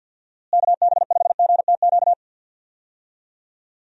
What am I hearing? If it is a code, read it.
Morse code, heard as a keyed tone, about 49 wpm, 700 Hz. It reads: KB5GTQ
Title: USA Call Signs - 6 Characters - Repeated 2x - 50wpm